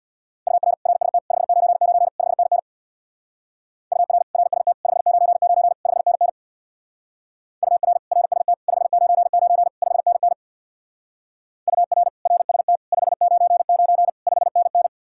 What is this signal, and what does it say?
Morse code, heard as a keyed tone, about 64 wpm, 700 Hz. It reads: UR RST 599 5NN UR RST 599 5NN UR RST 599 5NN UR RST 599 5NN